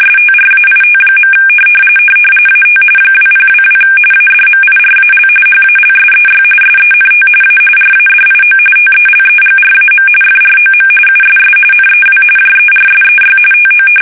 RTTY, about 45.45 baud. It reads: STRAIT. AT 2022 ADMIRAL HALSEY SENT A RADIO TELEGRAM FROM USS NEW JERSEY TO
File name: fsk850 / battleshipnj